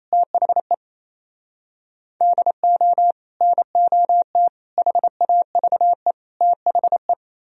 Morse code, heard as a keyed tone, about 28 wpm, 700 Hz. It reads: THE DO NOT HAVE THE
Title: Speed-Racing - Sets of 4 Words 28wpm